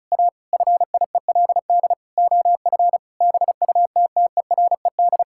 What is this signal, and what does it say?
Morse code, heard as a keyed tone, about 36 wpm, 700 Hz. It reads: A FIELD OF BUTTERED
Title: Pangram Challenge v2 - 35wpm